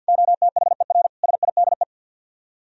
Morse code, heard as a keyed tone, about 64 wpm, 700 Hz. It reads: OTHER SIDE